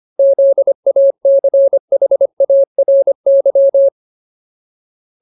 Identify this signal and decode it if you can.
Morse code, heard as a keyed tone, about 25 wpm, 550 Hz. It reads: ZACHARY
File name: Ultimate Word List / 25WPM Ultimate